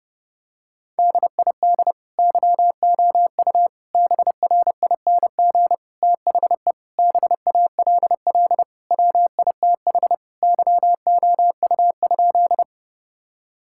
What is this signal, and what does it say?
Morse code, heard as a keyed tone, about 30 wpm, 700 Hz. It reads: DID YOU BRING THE BALL WITH YOU?
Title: Sentences from Top 1000 Words - Review Part 6 - 30wpm